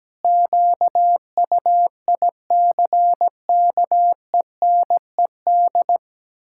Morse code, heard as a keyed tone, about 17 wpm, 700 Hz. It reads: QUICKENED
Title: Pangram Challenge v2 - 17wpm